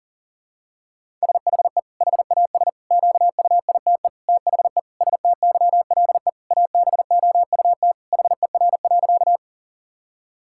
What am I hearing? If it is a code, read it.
Morse code, heard as a keyed tone, about 40 wpm, 700 Hz. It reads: SHE HAS QUITE THE STYLE ABOUT HER.
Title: Sentences from Top 1000 Words - Review Part 5 - 40wpm